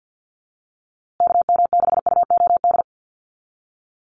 Morse code, heard as a keyed tone, about 50 wpm, 700 Hz. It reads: KM6UOD
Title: USA Call Signs Repeated 2x 50wpm